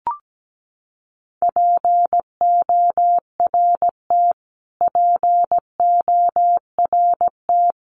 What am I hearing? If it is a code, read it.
Morse code, heard as a keyed tone, about 17 wpm, 700 Hz. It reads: PORT PORT
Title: Top 500 Words Repeated 17wpm